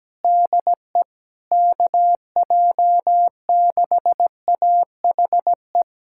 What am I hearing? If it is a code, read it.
Morse code, heard as a keyed tone, about 17 wpm, 700 Hz. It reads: DE KJ6AHE
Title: USA Callsign DE Callsign - 6 Characters - 17wpm